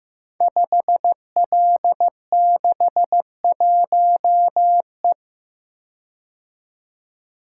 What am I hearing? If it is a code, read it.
Morse code, heard as a keyed tone, about 15 wpm, 700 Hz. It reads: 5L61E
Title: Code Groups - 5 Letter:Numbers - 15wpm